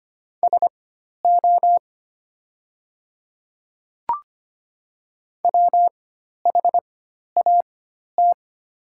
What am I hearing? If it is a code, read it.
Morse code, heard as a keyed tone, about 25 wpm, spaced as for 12 wpm, 700 Hz. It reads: SO WHAT
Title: 4x Character Spacing - Top 300 Words - 25wpm